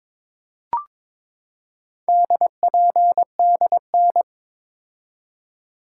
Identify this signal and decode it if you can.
Morse code, heard as a keyed tone, about 22 wpm, 700 Hz. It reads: DPDN
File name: practice / 22wpm / Four Letter-Number Combinations - 22wpm